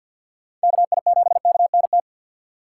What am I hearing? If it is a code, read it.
Morse code, heard as a keyed tone, about 47 wpm, 700 Hz. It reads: KI7KNT
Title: USA Call Signs - 6 Characters - 3x Character Spacing - 50wpm